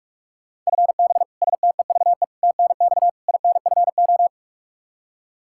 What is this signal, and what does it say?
Morse code, heard as a keyed tone, about 45 wpm, 700 Hz. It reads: FB STEVE TNX INFO
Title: Practice QSO - Exchange 3 - More Info - 45wpm